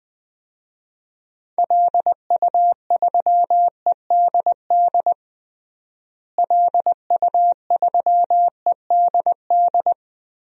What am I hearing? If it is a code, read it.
Morse code, heard as a keyed tone, about 20 wpm, 700 Hz. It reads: LU3EDD LU3EDD
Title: DXCC-100 Call Signs - 6 Characters - Repeated 2x - 20wpm